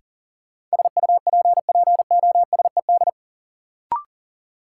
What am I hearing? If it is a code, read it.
Morse code, heard as a keyed tone, about 37 wpm, 700 Hz. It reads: SUPPOSED
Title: Sets of 3 Words - New Word Drill - 40wpm